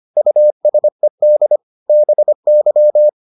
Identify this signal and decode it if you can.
Morse code, heard as a keyed tone, about 25 wpm, 600 Hz. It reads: USED BY